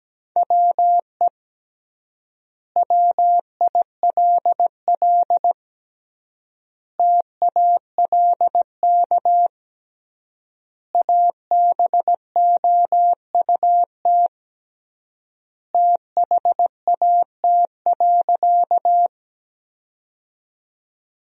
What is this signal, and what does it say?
Morse code, heard as a keyed tone, about 17 wpm, 700 Hz. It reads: WE WILL TALK ABOUT THAT.
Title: 3x Word-Spacing - Sentences from Top 400 Words - 17wpm